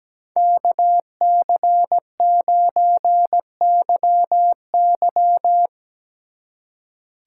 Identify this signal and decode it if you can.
Morse code, heard as a keyed tone, about 19 wpm, 700 Hz. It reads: KC9YY
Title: Speed Racing - USA 2x2 Call Signs 17wpm